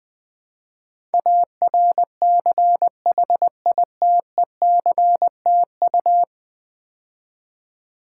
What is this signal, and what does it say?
Morse code, heard as a keyed tone, about 20 wpm, 700 Hz. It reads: ARCHITECTU
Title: Build A Word - 5 Letters At A Time - 12-Letter Words - 20wpm